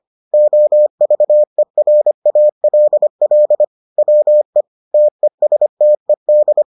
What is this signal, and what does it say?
Morse code, heard as a keyed tone, about 25 wpm, 600 Hz. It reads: OVERALL WE TESTED